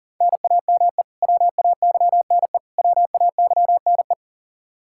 Morse code, heard as a keyed tone, about 40 wpm, 700 Hz. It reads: NAME WAYNE WAYNE
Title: Practice QSO - Exchange 1 - Beginning - 40wpm